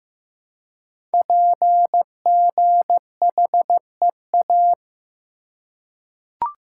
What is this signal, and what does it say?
Morse code, heard as a keyed tone, about 15 wpm, 700 Hz. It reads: PGHEA
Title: Code Groups - 5 Letters - 15wpm